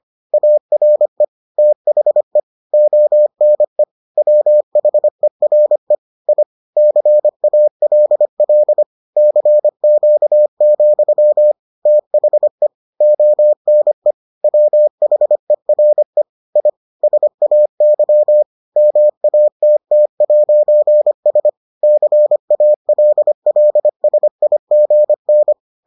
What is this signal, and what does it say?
Morse code, heard as a keyed tone, about 25 wpm, 600 Hz. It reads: ARE THE ONE WHERE I CALL CQ, THE ONE WHERE I SAY MATT'S CALLSIGN